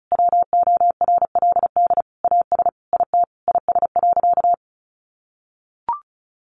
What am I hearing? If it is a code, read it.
Morse code, heard as a keyed tone, about 34 wpm, 700 Hz. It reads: WORLD AS IT IS.
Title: Sentences from Top 1200 Words 35wpm